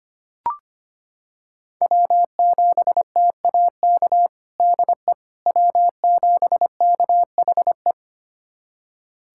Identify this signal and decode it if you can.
Morse code, heard as a keyed tone, about 25 wpm, 700 Hz. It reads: W7TAK DE W7KHE
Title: USA Callsign DE Callsign - 5 Characters - 25wpm